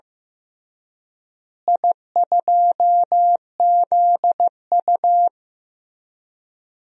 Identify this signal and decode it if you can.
Morse code, heard as a keyed tone, about 15 wpm, 700 Hz. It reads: I2ZU
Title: DXCC-100 Call Signs - 4 Characters - Repeated 3x - 15wpm